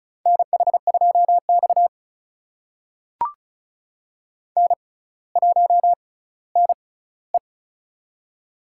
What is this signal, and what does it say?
Morse code, heard as a keyed tone, about 36 wpm, spaced as for 18 wpm, 700 Hz. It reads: NH2X N1NE
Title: USA Call Signs - 4 Characters - 6x Character Spacing - 35wpm